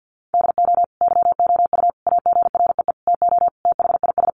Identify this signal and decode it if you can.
Morse code, heard as a keyed tone, about 50 wpm, 700 Hz. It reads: DO YOU AGREE TO THIS
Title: Sentences from Top 1100 Words 50wpm